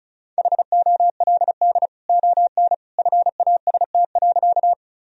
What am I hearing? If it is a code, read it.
Morse code, heard as a keyed tone, about 35 wpm, 700 Hz. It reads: HOLD ON FAST.
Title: Sentences from Top 1000 Words - Review Part 2 - 35wpm